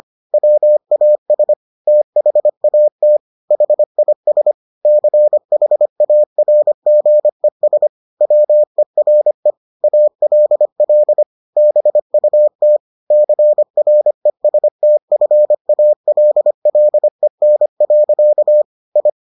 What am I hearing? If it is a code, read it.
Morse code, heard as a keyed tone, about 25 wpm, 600 Hz. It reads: WAS THAT HIS CHARGES WERE ALL BUT CRESTFALLEN. I